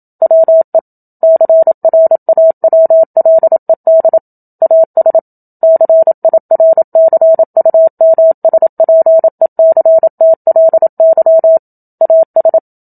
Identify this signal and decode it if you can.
Morse code, heard as a keyed tone, about 27 wpm, 650 Hz. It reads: WE CRAWLED AS CIRCUMSPECTLY AS
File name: Book Two / 01-Chapter 1 - 27 WPM